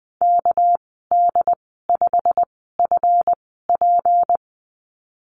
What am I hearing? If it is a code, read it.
Morse code, heard as a keyed tone, about 20 wpm, spaced as for 15 wpm, 700 Hz. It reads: KD5FP
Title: 219 Lesson 35g - Number 0 - Call Signs 20wpm